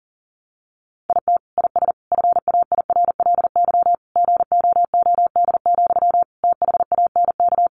Morse code, heard as a keyed tone, about 40 wpm, 700 Hz. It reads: IT IS FAIRLY GOOD, THANK
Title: Sentences from Top 1100 Words 40wpm